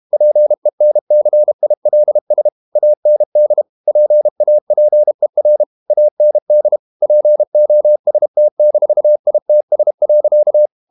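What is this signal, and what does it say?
Morse code, heard as a keyed tone, about 32 wpm, 600 Hz. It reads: PENCILS AND PAPER AND POST-ITS.
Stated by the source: Morse code practice at 32 wpm from CW med Gnister, reading Ken Follett's novel Never in English.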